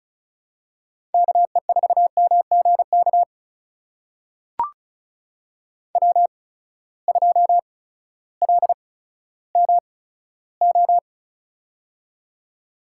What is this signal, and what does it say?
Morse code, heard as a keyed tone, about 36 wpm, spaced as for 16 wpm, 700 Hz. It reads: KE4MGK W2LMO
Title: USA Call Signs - 8x Character Spacing - 35wpm